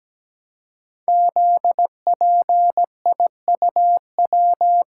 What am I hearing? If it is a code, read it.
Morse code, heard as a keyed tone, about 17 wpm, 700 Hz. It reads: ZPIUW
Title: Code Groups - 5 Letters - 17wpm